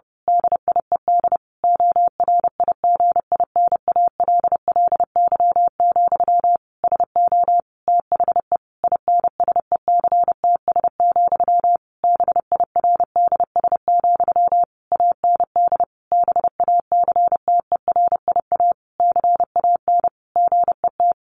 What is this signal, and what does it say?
Morse code, heard as a keyed tone, about 30 wpm, 700 Hz. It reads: DIED ORIGINALLY, SO THE INSECTS, BIRDS, AND BACTERIA CAN GET
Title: A Sound Of Thunder 30wpm